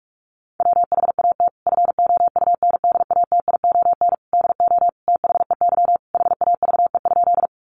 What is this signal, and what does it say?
Morse code, heard as a keyed tone, about 44 wpm, 700 Hz. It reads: WHAT FOUNDATION DO THEY HAVE?
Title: Sentences from Top 1200 Words 45wpm